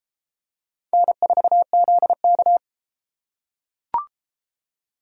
Morse code, heard as a keyed tone, about 32 wpm, 700 Hz. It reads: N4ZK